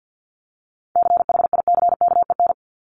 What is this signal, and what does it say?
Morse code, heard as a keyed tone, about 50 wpm, 700 Hz. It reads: CHICKEN
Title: Multi-letter Phonemes Repeated 3x 50wpm